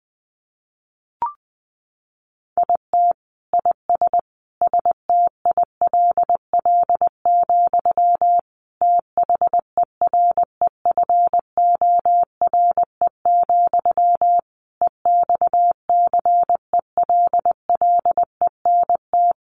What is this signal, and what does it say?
Morse code, heard as a keyed tone, about 20 wpm, 700 Hz. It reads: IT IS STILL, THEREFORE, EXCELLENT
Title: Sentences from Top 1000 Words 20wpm